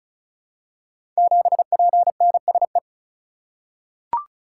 Morse code, heard as a keyed tone, about 36 wpm, 700 Hz. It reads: 7PNSE
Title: Code Groups - 5 Letter:Numbers - 35wpm